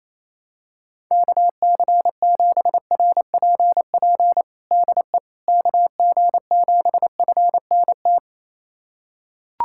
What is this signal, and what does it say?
Morse code, heard as a keyed tone, about 28 wpm, 700 Hz. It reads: KC7RPP DE KG7FNT
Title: USA Callsign DE Callsign - 6 Characters - 28wpm